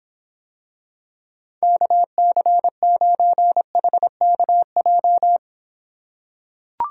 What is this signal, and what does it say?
Morse code, heard as a keyed tone, about 26 wpm, 700 Hz. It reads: KC9HKJ